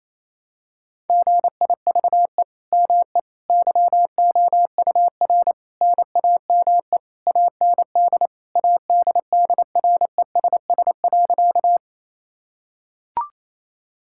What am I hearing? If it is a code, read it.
Morse code, heard as a keyed tone, about 28 wpm, 700 Hz. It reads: GIVE ME YOUR NAME AND ADDRESS.
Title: Sentences from Top 1000 Words - Review Part 6 - 28wpm